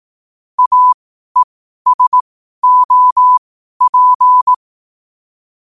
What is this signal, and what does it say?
Morse code, heard as a keyed tone, about 18 wpm, spaced as for 12 wpm, 1000 Hz. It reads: AESOP
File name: qotdfiles / 12wpm